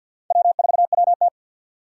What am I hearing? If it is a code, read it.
Morse code, heard as a keyed tone, about 47 wpm, 700 Hz. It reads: W4WT
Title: USA Call Signs - 4x Character Spacing - 50wpm